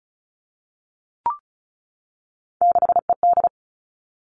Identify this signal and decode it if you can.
Morse code, heard as a keyed tone, about 36 wpm, 700 Hz. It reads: 6ED
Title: Three Challenging Letter-Number Combinations 35wpm